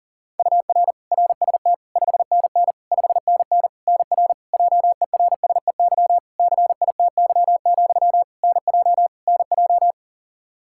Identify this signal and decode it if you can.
Morse code, heard as a keyed tone, about 40 wpm, 700 Hz. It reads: UR RST 5NN 5NN NR JERSEY CITY, NJ NJ
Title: Practice QSO - Exchange 1 - Beginning - 40wpm